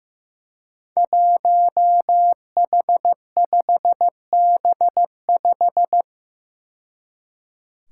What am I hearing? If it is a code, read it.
Morse code, heard as a keyed tone, about 15 wpm, 700 Hz. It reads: 1H5B5
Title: Five Challenging Letter-Number Combinations 15wpm